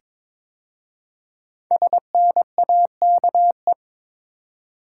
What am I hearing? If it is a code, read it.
Morse code, heard as a keyed tone, about 22 wpm, 700 Hz. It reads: SNAKE
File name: 5 Letter Words - First 3 Letters Given - 22wpm